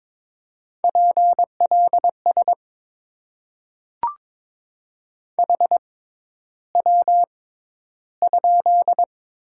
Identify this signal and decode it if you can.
Morse code, heard as a keyed tone, about 22 wpm, spaced as for 11 wpm, 700 Hz. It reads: PLS HW?
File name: Top 90 QSO Elements - 6x Character Spacing - 22wpm